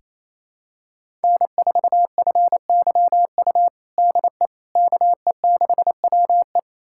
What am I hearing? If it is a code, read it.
Morse code, heard as a keyed tone, about 28 wpm, 700 Hz. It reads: N4FYU DE KE6WE
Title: USA Callsign DE Callsign - 5 Characters - 28wpm